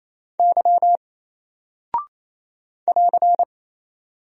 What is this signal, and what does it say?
Morse code, heard as a keyed tone, about 27 wpm, 700 Hz. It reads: Y <AR>
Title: Single Character (Letter:Number:Punctuation:Prosign) 28wpm